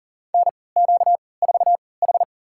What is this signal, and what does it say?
Morse code, heard as a keyed tone, about 40 wpm, spaced as for 23 wpm, 700 Hz. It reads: NQ4H
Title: USA Call Signs - 4 Characters - 3x Character Spacing - 40wpm